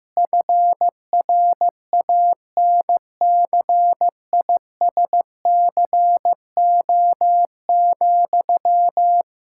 Morse code, heard as a keyed tone, about 15 wpm, 700 Hz. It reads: FRANCISCO,
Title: Practice QSO - Exchange 1 - Beginning - 15wpm